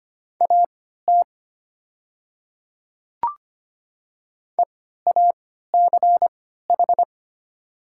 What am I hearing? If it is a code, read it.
Morse code, heard as a keyed tone, about 25 wpm, spaced as for 14 wpm, 700 Hz. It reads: AT EACH
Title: 3x Character Spacing - Top 100 Words - 25wpm